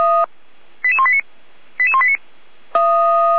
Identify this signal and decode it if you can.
Five-tone selective call (ZVEI), heard as quick runs of five tones, tones of about 70 ms. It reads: 88189 88189